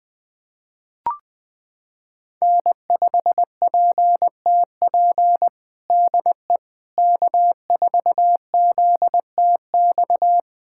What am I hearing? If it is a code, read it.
Morse code, heard as a keyed tone, about 20 wpm, 700 Hz. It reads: N5PTP DE K4ZTX
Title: USA Callsign DE Callsign - 5 Characters - 20wpm